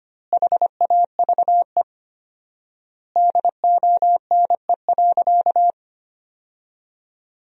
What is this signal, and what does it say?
Morse code, heard as a keyed tone, about 25 wpm, 700 Hz. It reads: HAVE DONE.
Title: 4x Word-Spacing - Sentences from Top 500 Words - 25wpm